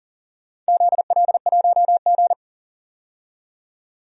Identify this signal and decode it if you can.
Morse code, heard as a keyed tone, about 40 wpm, 700 Hz. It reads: ZL1G
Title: DXCC-100 Call Signs - 4 Characters - 40wpm